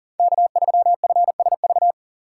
Morse code, heard as a keyed tone, about 40 wpm, 700 Hz. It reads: K3FSV
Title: USA Call Signs - 5 Characters - 2x Character Spacing - 40wpm